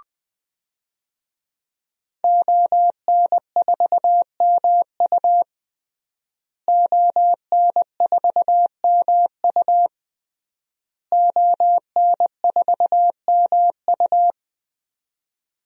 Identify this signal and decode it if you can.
Morse code, heard as a keyed tone, about 20 wpm, 700 Hz. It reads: ON4MU ON4MU ON4MU
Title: DXCC-100 Call Signs - 5 Characters - Repeated 3x - 20wpm